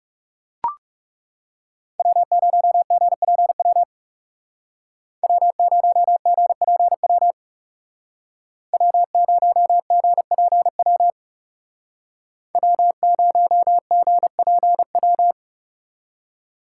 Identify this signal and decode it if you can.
Morse code, heard as a keyed tone, about 35 wpm, 700 Hz. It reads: W0GPW W0GPW W0GPW W0GPW
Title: Speed Racing - USA 1x3 Call Signs 30wpm